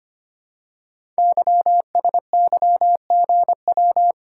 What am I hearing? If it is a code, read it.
Morse code, heard as a keyed tone, about 25 wpm, 700 Hz. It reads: YSYGW
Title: Code Groups - 5 Letters - 25wpm